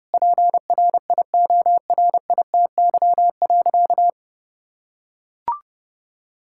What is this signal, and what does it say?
Morse code, heard as a keyed tone, about 32 wpm, 700 Hz. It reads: PRIORITY.
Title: Sentences from Top 1600 Words 30wpm